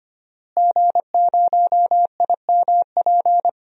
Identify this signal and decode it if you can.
Morse code, heard as a keyed tone, about 26 wpm, 700 Hz. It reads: G0IMP
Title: DXCC-100 Call Signs - 5 Characters - Repeated 2x - 25wpm